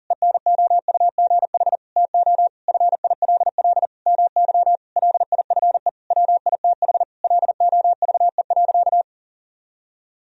Morse code, heard as a keyed tone, about 40 wpm, 700 Hz. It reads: ENOUGH TO FILL MY LIFE WITH LOVE.
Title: Sentences from Top 1000 Words - Review Part 3 - 40wpm